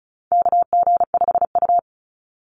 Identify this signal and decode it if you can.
Morse code, heard as a keyed tone, about 34 wpm, 700 Hz. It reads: KG5U
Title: Call Signs Repeated 3x - 35wpm